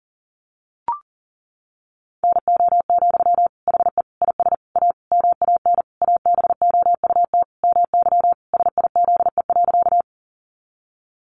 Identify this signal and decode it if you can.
Morse code, heard as a keyed tone, about 37 wpm, 700 Hz. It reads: NO, HE IS A MAN ABOUT MY SIZE.
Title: Sentences from Top 400 Words 40wpm